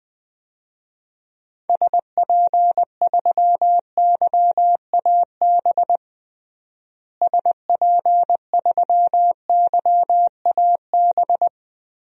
Morse code, heard as a keyed tone, about 20 wpm, 700 Hz. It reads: SP3YAB SP3YAB
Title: DXCC-100 Call Signs - 6 Characters - Repeated 2x - 20wpm